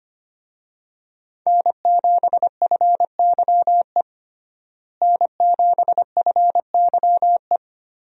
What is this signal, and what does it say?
Morse code, heard as a keyed tone, about 25 wpm, 700 Hz. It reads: N7FYE N7FYE
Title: USA Call Signs - 5 Characters - Repeated 2x - 25wpm